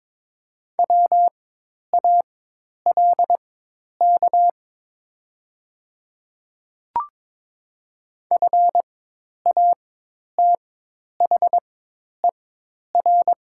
Morse code, heard as a keyed tone, about 22 wpm, spaced as for 10 wpm, 700 Hz. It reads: WALK FATHER
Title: 4x Character Spacing - Top 300 Words - 22wpm